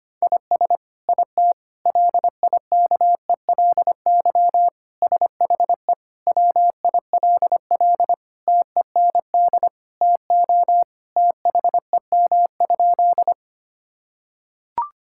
Morse code, heard as a keyed tone, about 25 wpm, 700 Hz. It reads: IS IT LIKELY SHE WILL TEND TO THEM?
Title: Sentences from Top 1000 Words - Review Part 5 - 25wpm